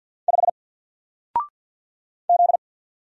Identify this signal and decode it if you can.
Morse code, heard as a keyed tone, about 48 wpm, 700 Hz. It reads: 5 Z
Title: Single Letter-Number Alt 50wpm